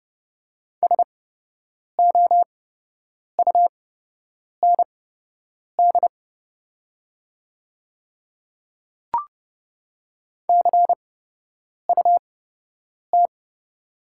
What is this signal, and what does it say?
Morse code, heard as a keyed tone, about 30 wpm, spaced as for 8 wpm, 700 Hz. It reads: SOUND CUT
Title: 8x Character Spacing - Top 300 Words - 30wpm